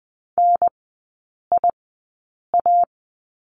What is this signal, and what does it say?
Morse code, heard as a keyed tone, about 20 wpm, spaced as for 8 wpm, 700 Hz. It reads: NIA
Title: 041 Lesson 8e - Number 1 - Three Characters 20wpm